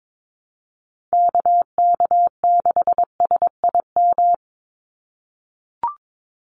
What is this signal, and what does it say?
Morse code, heard as a keyed tone, about 22 wpm, 700 Hz. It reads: KK6SIM
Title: USA Call Signs 22wpm